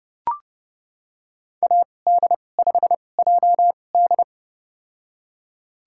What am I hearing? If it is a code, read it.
Morse code, heard as a keyed tone, about 32 wpm, spaced as for 22 wpm, 700 Hz. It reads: AD5JD
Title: USA Call Signs - 5 Characters - 2x Character Spacing - 30wpm